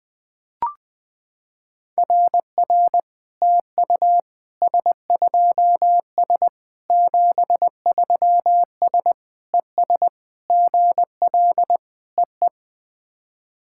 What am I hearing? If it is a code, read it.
Morse code, heard as a keyed tone, about 20 wpm, 700 Hz. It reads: RR TU S2S 73S ES GL EE
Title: SOTA QSO Practice 20wpm